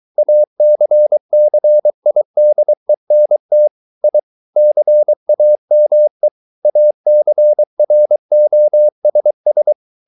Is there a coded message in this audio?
Yes: Morse code from 0.2 to 9.7 s, heard as a keyed tone, about 23 wpm, 600 Hz. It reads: ACCIDENT I CAME ACROSS